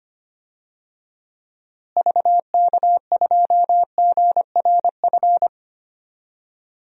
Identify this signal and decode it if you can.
Morse code, heard as a keyed tone, about 25 wpm, 700 Hz. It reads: VK2GRF
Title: DXCC-100 Call Signs - 6 Characters - Repeated 3x - 25wpm